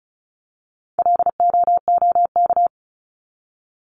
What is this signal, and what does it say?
Morse code, heard as a keyed tone, about 34 wpm, 700 Hz. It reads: LOOK
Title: ING Suffix - Top 100 Words with Base Words 35wpm